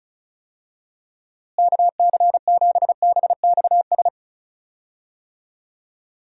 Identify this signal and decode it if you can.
Morse code, heard as a keyed tone, about 34 wpm, 700 Hz. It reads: KC7BXS
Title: USA Call Signs - 6 Characters - Repeated 3x - 35wpm